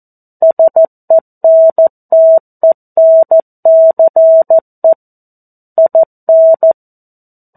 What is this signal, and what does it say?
Morse code, heard as a keyed tone, about 14 wpm, 650 Hz. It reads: SENTENCE IN
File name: 02-Chapter-02 14 WPM